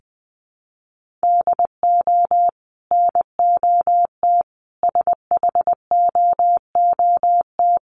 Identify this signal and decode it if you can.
Morse code, heard as a keyed tone, about 20 wpm, 700 Hz. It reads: DO NOT SHOOT
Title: Sentences from Top 1200 Words 20wpm